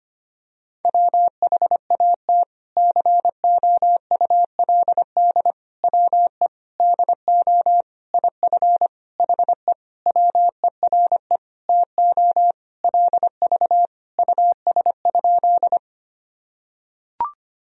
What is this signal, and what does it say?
Morse code, heard as a keyed tone, about 25 wpm, 700 Hz. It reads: WHAT COULD WE DO IF HE WERE TO LV US?
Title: Sentences from Top 400 Words and Abbreviations 25wpm